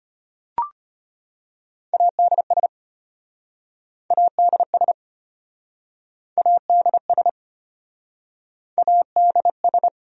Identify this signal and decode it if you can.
Morse code, heard as a keyed tone, about 33 wpm, 700 Hz. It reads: ADS ADS ADS ADS